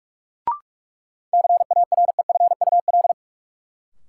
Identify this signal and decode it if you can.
Morse code, heard as a keyed tone, about 46 wpm, 700 Hz. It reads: CAREFUL
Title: Sentences from Top 1600 Words - New Word Drill Alt - 45wpm